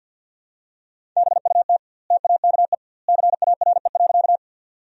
Morse code, heard as a keyed tone, about 50 wpm, 700 Hz. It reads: BUT TAKE CARE.
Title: 2x Word-Spacing - Sentences from Top 300 Words - 50wpm